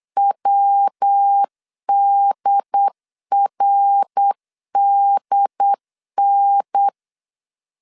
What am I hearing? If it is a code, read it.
Morse code, heard as a keyed tone, about 8 wpm, 800 Hz. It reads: WDRDN